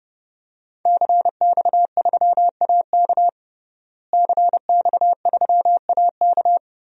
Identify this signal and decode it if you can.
Morse code, heard as a keyed tone, about 30 wpm, 700 Hz. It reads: CX3AK CX3AK
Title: DXCC-100 Call Signs - 5 Characters - Repeated 2x - 30wpm